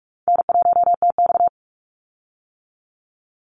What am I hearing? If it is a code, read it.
Morse code, heard as a keyed tone, about 46 wpm, 700 Hz. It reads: N1TX
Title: Easy Call Signs - 45wpm